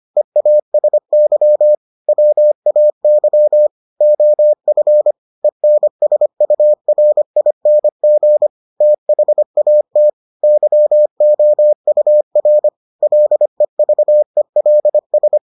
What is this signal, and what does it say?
Morse code, heard as a keyed tone, about 25 wpm, 600 Hz. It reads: EASY WAY OF ENSURING THAT YOUR LEVELS